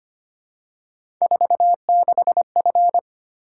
Morse code, heard as a keyed tone, about 25 wpm, 700 Hz. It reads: 46F
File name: Three Letter-Number Combinations - 25wpm